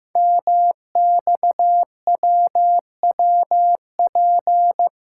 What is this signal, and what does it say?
Morse code, heard as a keyed tone, about 15 wpm, 700 Hz. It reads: MXWWP
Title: Code Groups - 5 Letters - 15wpm